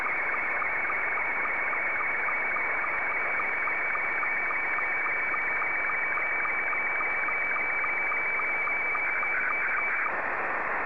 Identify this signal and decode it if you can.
SSTV (slow-scan television), Scottie 2.